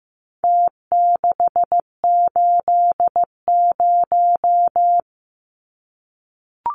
Morse code, heard as a keyed tone, about 15 wpm, 700 Hz. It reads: T680
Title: Numbers - Contest Style - 4-Digit 15wpm